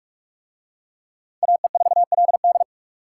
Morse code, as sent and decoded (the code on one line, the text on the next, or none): .- . ....- .-.. -..
AE4LD